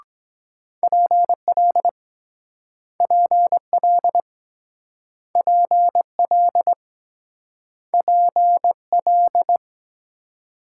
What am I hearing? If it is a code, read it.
Morse code, heard as a keyed tone, about 22 wpm, 700 Hz. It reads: PL PL PL PL